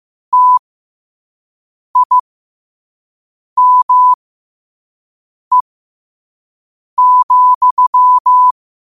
Morse code, heard as a keyed tone, about 15 wpm, spaced as for 5 wpm, 1000 Hz. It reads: TIME,
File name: 5wpm